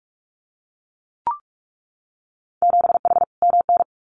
Morse code, heard as a keyed tone, about 44 wpm, 690 Hz. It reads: 7H MN
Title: Winter Field Day Exchanges Encore 45wpm